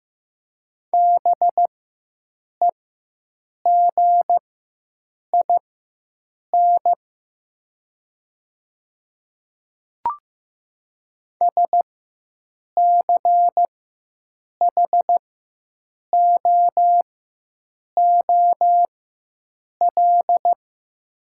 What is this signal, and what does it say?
Morse code, heard as a keyed tone, about 15 wpm, spaced as for 7 wpm, 700 Hz. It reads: BEGIN SCHOOL
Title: 4x Character Spacing - Top 300 Words - 15wpm